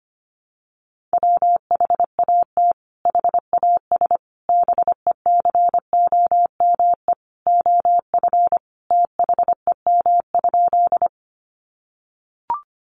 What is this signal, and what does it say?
Morse code, heard as a keyed tone, about 25 wpm, 700 Hz. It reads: WHAT HAS BECOME OF THEM?
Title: Sentences from Top 600 Words 25wpm